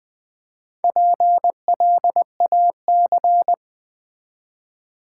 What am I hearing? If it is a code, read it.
Morse code, heard as a keyed tone, about 20 wpm, 700 Hz. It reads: PLAC
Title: Build A Word - 1 Letter At A Time - 5-Letter Words - 20wpm